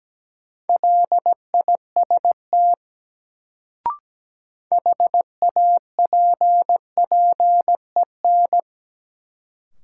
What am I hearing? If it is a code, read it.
Morse code, heard as a keyed tone, about 17 wpm, 700 Hz. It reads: LIST HAPPEN
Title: Sentences from Top 400 Words - New Word Drill - 17wpm